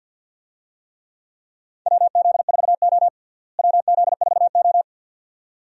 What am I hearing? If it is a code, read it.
Morse code, heard as a keyed tone, about 48 wpm, 700 Hz. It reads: WZ4O WZ4O
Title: USA Call Signs - 4 Characters - Repeated 2x - 50wpm